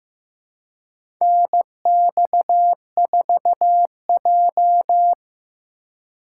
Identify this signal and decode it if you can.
Morse code, heard as a keyed tone, about 15 wpm, 700 Hz. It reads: NX4J
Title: USA Call Signs - 4x Character Spacing - 15wpm